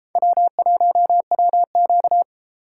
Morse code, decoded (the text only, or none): W1WQ